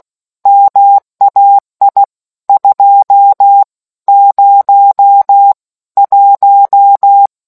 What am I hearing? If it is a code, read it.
Morse code, heard as a keyed tone, about 16 wpm, spaced as for 13 wpm, 800 Hz. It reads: MAI 201